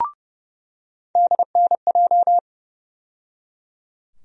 Morse code, heard as a keyed tone, about 30 wpm, 700 Hz. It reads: DNJ
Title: USA Call Sign Postfixes - 3 Letters 30wpm